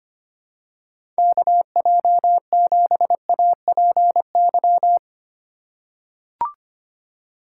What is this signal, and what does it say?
Morse code, heard as a keyed tone, about 25 wpm, 700 Hz. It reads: KJ7APY
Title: USA Call Signs - 6 Characters - 2x Character Spacing - 25wpm